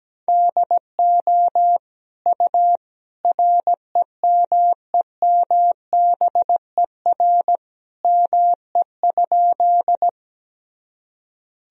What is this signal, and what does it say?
Morse code, heard as a keyed tone, about 17 wpm, 700 Hz. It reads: DO U REMEMBER ME?
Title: Sentences from Top 400 Words and Abbreviations 17wpm